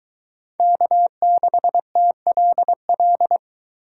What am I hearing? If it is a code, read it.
Morse code, heard as a keyed tone, about 22 wpm, 700 Hz. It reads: K6TLL